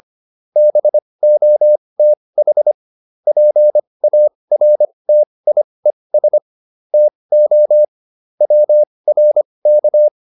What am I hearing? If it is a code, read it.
Morse code, heard as a keyed tone, about 25 wpm, spaced as for 20 wpm, 600 Hz. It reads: BOTH PARTIES TO WRK